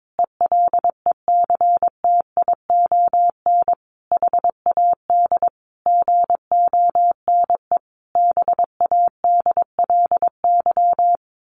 Morse code, heard as a keyed tone, about 22 wpm, 700 Hz. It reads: ELECTION HAD GONE BADLY
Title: A Sound Of Thunder 22wpm